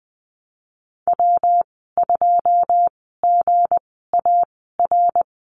Morse code, heard as a keyed tone, about 19 wpm, spaced as for 14 wpm, 700 Hz. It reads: W2GAR
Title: 167 Lesson 27g - Number 7 - Call Signs 20wpm